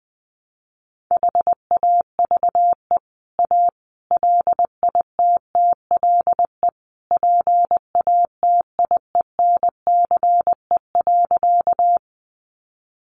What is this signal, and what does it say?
Morse code, heard as a keyed tone, about 19 wpm, 700 Hz. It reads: HAVE A LITTLE PATIENCE.
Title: Sentences from Top 1300 Words 20wpm